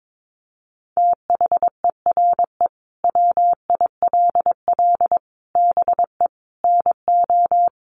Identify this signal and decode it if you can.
Morse code, heard as a keyed tone, about 22 wpm, 700 Hz. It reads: THERE WILL BE NO
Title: Sentences from Top 1500 Words 22wpm